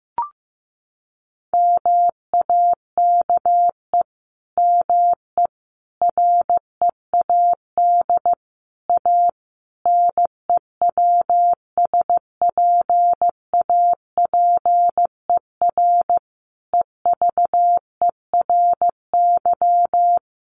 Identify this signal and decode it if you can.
Morse code, heard as a keyed tone, about 15 wpm, 700 Hz. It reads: MAKE ME READ A NEWSPAPER EVERY
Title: Sentences from Top 1100 Words 15wpm